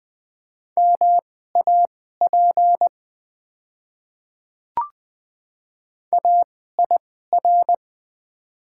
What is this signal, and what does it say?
Morse code, heard as a keyed tone, about 20 wpm, spaced as for 14 wpm, 700 Hz. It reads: MAP AIR
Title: 2x Character Spacing - Top 500 Words - 20wpm